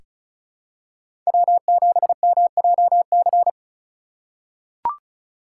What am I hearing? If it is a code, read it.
Morse code, heard as a keyed tone, about 36 wpm, 700 Hz. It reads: W7MJC
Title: USA Call Signs - 5 Characters - 2x Character Spacing - 35wpm